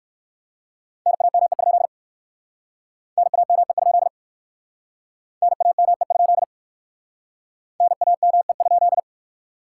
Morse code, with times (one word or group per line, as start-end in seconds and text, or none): 1.1-1.9 NAME?
3.2-4.1 NAME?
5.4-6.4 NAME?
7.8-9.0 NAME?